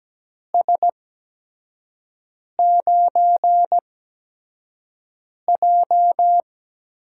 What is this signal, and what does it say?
Morse code, heard as a keyed tone, about 17 wpm, spaced as for 5 wpm, 700 Hz. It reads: S9J
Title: USA Call Signs - 4 Characters - 8x Character Spacing - 17wpm